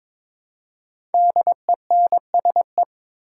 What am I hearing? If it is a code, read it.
Morse code, heard as a keyed tone, about 22 wpm, 690 Hz. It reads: DENSE
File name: 5 Letter Words - First 3 Letters Given - 22wpm